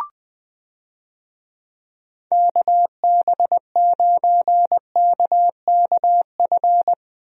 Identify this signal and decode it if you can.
Morse code, heard as a keyed tone, about 21 wpm, 700 Hz. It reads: KB9KKF
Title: USA Call Signs - 6 Characters - Repeated 2x - 20wpm